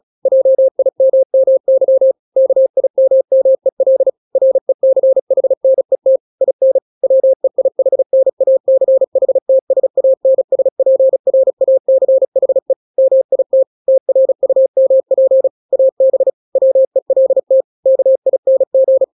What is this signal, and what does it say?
Morse code, heard as a keyed tone, about 36 wpm, 530 Hz. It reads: JIMMY KIMMEL RECHNET IN WEIHNACHTSANSPRACHE MIT TRUMP AB WELT KING
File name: daily / 35wpm